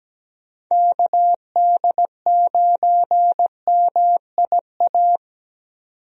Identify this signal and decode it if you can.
Morse code, heard as a keyed tone, about 17 wpm, 700 Hz. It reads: KD9MIA
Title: USA Call Signs - 6 Characters - 17wpm